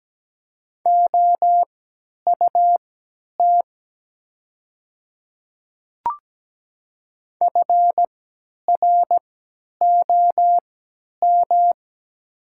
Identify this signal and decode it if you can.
Morse code, heard as a keyed tone, about 17 wpm, spaced as for 10 wpm, 700 Hz. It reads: OUT FROM
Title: 3x Character Spacing - Top 100 Words - 17wpm